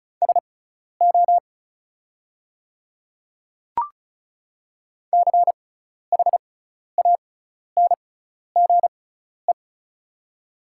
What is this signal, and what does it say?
Morse code, heard as a keyed tone, about 34 wpm, spaced as for 12 wpm, 700 Hz. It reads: SO CHANGE
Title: 6x Character Spacing - Top 200 Words - 35wpm